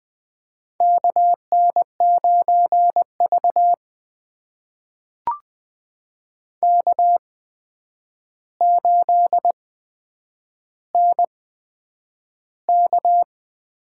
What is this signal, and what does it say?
Morse code, heard as a keyed tone, about 20 wpm, spaced as for 9 wpm, 700 Hz. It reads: KN9V K8NK
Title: USA Call Signs - 8x Character Spacing - 20wpm